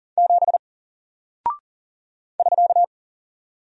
Morse code, heard as a keyed tone, about 40 wpm, 700 Hz. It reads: 7 <SK>
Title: Single Character (Letter:Number:Punctuation:Prosign) Alt 40wpm